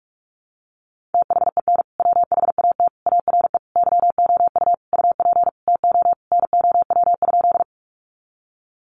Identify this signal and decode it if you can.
Morse code, heard as a keyed tone, about 44 wpm, 700 Hz. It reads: THEN WHAT ARE YOU UP TO NOW?
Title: Sentences from Top 100 Words 45wpm